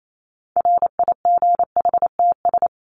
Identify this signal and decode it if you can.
Morse code, heard as a keyed tone, about 28 wpm, 700 Hz. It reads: RIGHTS
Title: ES:S Suffix - Top 100 Words with Base Words 28wpm